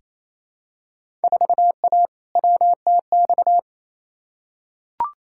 Morse code, heard as a keyed tone, about 29 wpm, 700 Hz. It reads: 4A WTX
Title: Field Day Exchanges Encore - 28wpm